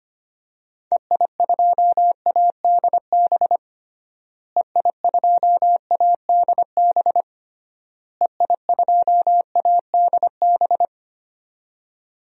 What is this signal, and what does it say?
Morse code, heard as a keyed tone, about 25 wpm, 700 Hz. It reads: EI2ADB EI2ADB EI2ADB
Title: DXCC-100 Call Signs - 6 Characters - Repeated 3x - 25wpm